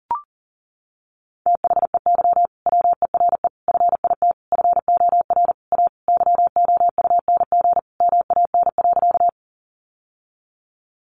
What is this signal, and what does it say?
Morse code, heard as a keyed tone, about 40 wpm, 700 Hz. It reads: THEY WERE FIT FOR A YOUNG MAN.
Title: Sentences from Top 800 Words 40wpm